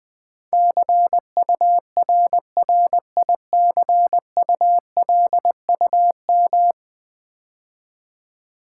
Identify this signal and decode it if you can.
Morse code, heard as a keyed tone, about 20 wpm, 700 Hz. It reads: CURRICULUM
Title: 10 Letter Words - First 3 Letters Given - 20wpm